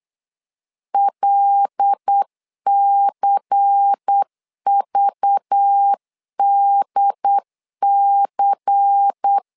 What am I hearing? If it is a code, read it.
Morse code, heard as a keyed tone, about 8 wpm, 800 Hz. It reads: LCVDC
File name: LEZIONI / 5 LBVC / 5 LBVC 8wpm 2935